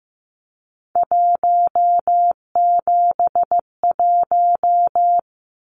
Morse code, heard as a keyed tone, about 15 wpm, 700 Hz. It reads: 171
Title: Numbers - 3-Digit 15wpm